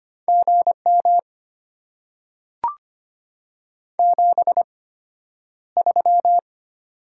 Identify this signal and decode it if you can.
Morse code, heard as a keyed tone, about 25 wpm, spaced as for 11 wpm, 690 Hz. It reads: GM 73
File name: Top 90 QSO Elements - 8x Character Spacing - 25wpm